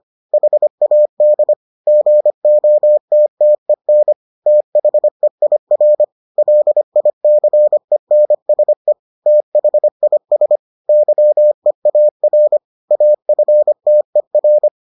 Morse code, heard as a keyed tone, about 25 wpm, 600 Hz. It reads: HAD GOTTEN THEIR LICENSE THIS YEAR AFTER